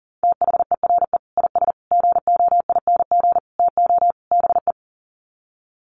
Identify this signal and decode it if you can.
Morse code, heard as a keyed tone, about 40 wpm, 700 Hz. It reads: THERE IS GOING TO BE
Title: Sets of 5 Words Encore 40wpm